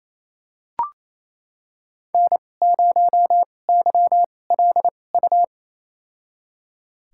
Morse code, heard as a keyed tone, about 28 wpm, spaced as for 20 wpm, 700 Hz. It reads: N0YLU
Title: USA Call Signs - 5 Characters - 2x Character Spacing - 28wpm